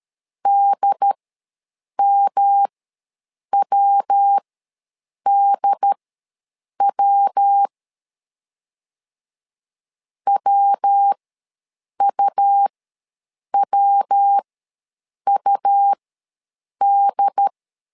Morse code, heard as a keyed tone, about 13 wpm, spaced as for 7 wpm, 800 Hz. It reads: DMWDW WUWUD
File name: LEZIONI / 3 MWRO / 3 MW ws 12wpm 1816